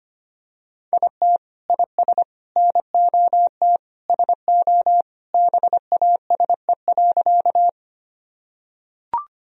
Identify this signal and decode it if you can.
Morse code, heard as a keyed tone, about 25 wpm, 700 Hz. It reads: IT IS NOT SO BASE.
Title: Sentences from Top 1000 Words - Review Part 1 - 25wpm